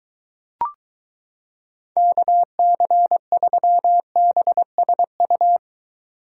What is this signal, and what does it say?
Morse code, heard as a keyed tone, about 22 wpm, 700 Hz. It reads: KC3BSU